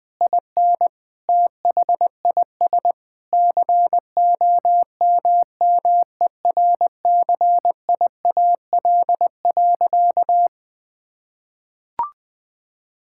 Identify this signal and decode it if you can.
Morse code, heard as a keyed tone, about 21 wpm, 700 Hz. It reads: IN THIS COMMERCIAL.
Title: Sentences from Top 1000 Words - Review Part 5 - 20wpm